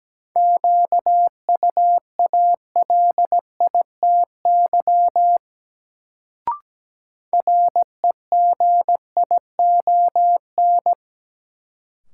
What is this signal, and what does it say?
Morse code, heard as a keyed tone, about 17 wpm, 700 Hz. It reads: QUALITY REGION
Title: Sentences from Top 1500 Words - New Word Drill Alt - 17wpm